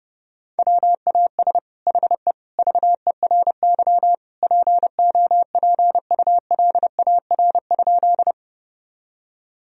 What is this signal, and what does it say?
Morse code, heard as a keyed tone, about 32 wpm, 700 Hz. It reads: WAS HE VERY POPULAR?
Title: Sentences from Top 1000 Words - Review Part 3 - 30wpm